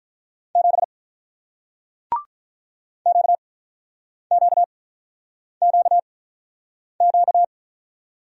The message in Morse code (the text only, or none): 7 QQQQ